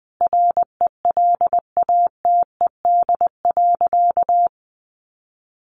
Morse code, heard as a keyed tone, about 20 wpm, 700 Hz. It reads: RELATED.
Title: Sentences from Top 1400 Words 20wpm